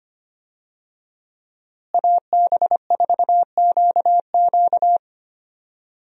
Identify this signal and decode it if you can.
Morse code, heard as a keyed tone, about 25 wpm, 700 Hz. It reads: AB4QQ
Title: USA Call Signs - 5 Characters - Repeated 2x - 25wpm